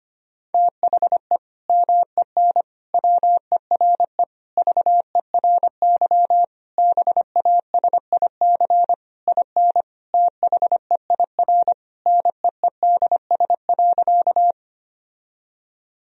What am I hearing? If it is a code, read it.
Morse code, heard as a keyed tone, about 25 wpm, 700 Hz. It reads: THE MEN WERE VERY BASIC IN THEIR NEEDS.
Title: Sentences from Top 1000 Words - Review Part 3 - 25wpm